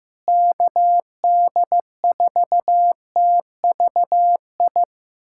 Morse code, heard as a keyed tone, about 15 wpm, 700 Hz. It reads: KD4TVI
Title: USA Callsign DE Callsign 15wpm